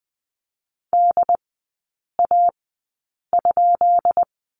Morse code, heard as a keyed tone, about 20 wpm, spaced as for 8 wpm, 700 Hz. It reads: DA?
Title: 133 Lesson 22e - Question Mark - Three Characters 20wpm